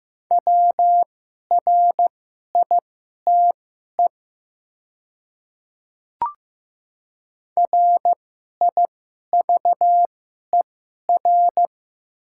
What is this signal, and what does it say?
Morse code, heard as a keyed tone, about 15 wpm, spaced as for 11 wpm, 700 Hz. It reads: WRITE RIVER
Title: 2x Character Spacing - Top 300 Words - 15wpm